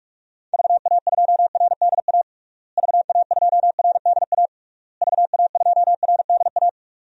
Morse code, heard as a keyed tone, about 46 wpm, 700 Hz. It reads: VA2RDA VA2RDA VA2RDA
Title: DXCC-100 Call Signs - 6 Characters - Repeated 3x - 45wpm